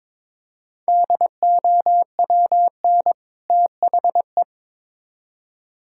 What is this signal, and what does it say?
Morse code, heard as a keyed tone, about 21 wpm, 700 Hz. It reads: DOWN THE
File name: Build A Phrase - 2-Word Phrases - 22wpm